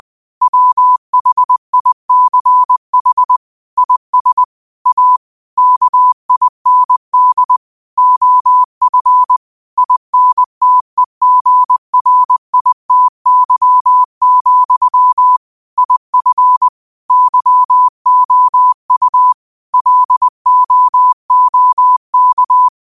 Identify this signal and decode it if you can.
Morse code, heard as a keyed tone, about 20 wpm, 1000 Hz. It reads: WHICH IS A KIND OF INTEGRITY, IF YOU LOOK